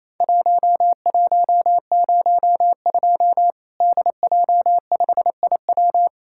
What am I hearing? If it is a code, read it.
Morse code, heard as a keyed tone, about 28 wpm, 700 Hz. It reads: 1102 DJ5IW
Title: Worked All Europe DX Contest - QTC Exchange - 28wpm